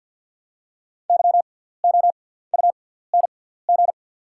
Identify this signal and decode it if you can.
Morse code, heard as a keyed tone, about 48 wpm, spaced as for 17 wpm, 700 Hz. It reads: YOUNG
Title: 6x Character Spacing - Top 500 Words - 50wpm